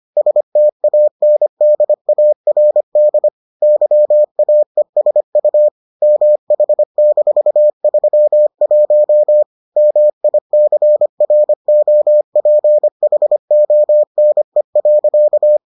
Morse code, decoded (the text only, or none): STANDARD YAESU MH-31 MICROPHONE.